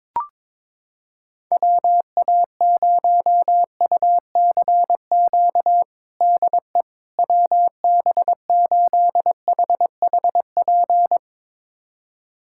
Morse code, heard as a keyed tone, about 22 wpm, 700 Hz. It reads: WA0UCQ DE WB8HHP
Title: USA Callsign DE Callsign - 6 Characters - 22wpm